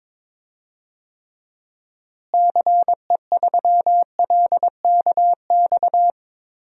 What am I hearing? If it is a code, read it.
Morse code, heard as a keyed tone, about 22 wpm, 700 Hz. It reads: CE3LKX
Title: DXCC-100 Call Signs - 6 Characters - Repeated 2x - 22wpm